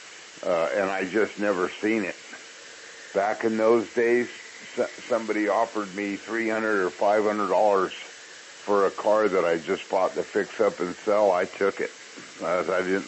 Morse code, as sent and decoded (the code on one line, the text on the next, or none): none